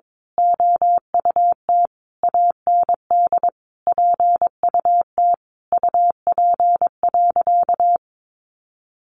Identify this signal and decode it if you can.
Morse code, heard as a keyed tone, about 22 wpm, 700 Hz. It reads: OUT AND PUT UP.
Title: A Sound Of Thunder 22wpm